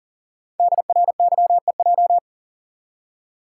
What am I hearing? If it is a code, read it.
Morse code, heard as a keyed tone, about 42 wpm, 690 Hz. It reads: DRYEJ
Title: Code Groups - 5 Letters - 40wpm